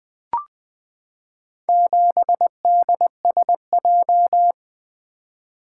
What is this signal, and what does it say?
Morse code, heard as a keyed tone, about 20 wpm, 700 Hz. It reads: 7DSJ
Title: Four Letter-Number Combinations - 20wpm